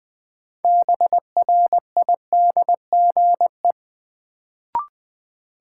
Morse code, heard as a keyed tone, about 20 wpm, 700 Hz. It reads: BRIDGE
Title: Sentences from Top 1200 Words - New Word Drill Alt - 20wpm